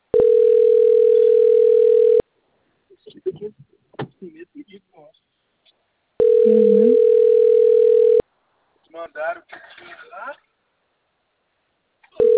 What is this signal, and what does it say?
Ringback tone (North America), 440 + 480 Hz, with speech.
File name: rg-1600-unknown- - - .4988